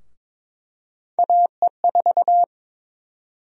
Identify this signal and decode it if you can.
Morse code, heard as a keyed tone, about 22 wpm, 700 Hz. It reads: AE4
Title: USA Call Sign Prefixes - 22wpm